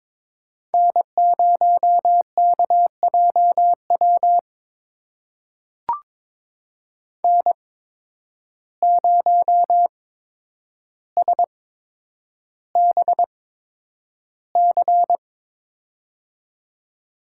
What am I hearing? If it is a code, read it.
Morse code, heard as a keyed tone, about 23 wpm, spaced as for 10 wpm, 700 Hz. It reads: N0KJW N0SBC
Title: USA Call Signs - 5 Characters - 8x Character Spacing - 22wpm